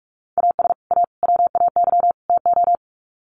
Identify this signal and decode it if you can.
Morse code, heard as a keyed tone, about 44 wpm, 700 Hz. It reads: AS A WAY TO
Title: Sets of 4 Words 45wpm